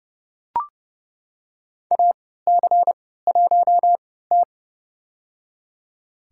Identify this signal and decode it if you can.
Morse code, heard as a keyed tone, about 30 wpm, spaced as for 17 wpm, 700 Hz. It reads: AC1T
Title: USA Call Signs - 4 Characters - 3x Character Spacing - 30wpm